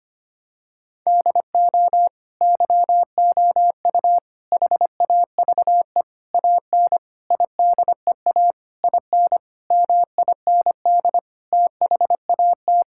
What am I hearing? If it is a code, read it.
Morse code, heard as a keyed tone, about 25 wpm, 700 Hz. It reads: DO YOU HAVE AN IDEA IN MIND THAT
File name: practice / 25wpm / Sentences with 9-Word Phrases - 25wpm